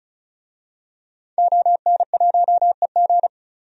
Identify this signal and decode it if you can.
Morse code, heard as a keyed tone, about 36 wpm, 700 Hz. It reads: ON1EG
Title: DXCC-100 Call Signs - 5 Characters - 35wpm